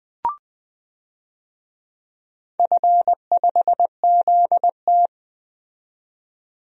Morse code, heard as a keyed tone, about 20 wpm, 700 Hz. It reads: F5ZT
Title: DXCC-100 Call Signs - 4 Characters - 20wpm